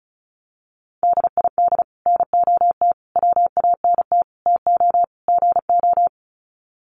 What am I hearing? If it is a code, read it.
Morse code, heard as a keyed tone, about 34 wpm, 700 Hz. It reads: DID NOT WANT TO GO
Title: Sets of 5 Words 35wpm